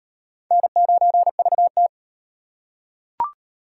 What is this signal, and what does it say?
Morse code, heard as a keyed tone, about 38 wpm, 700 Hz. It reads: N9VT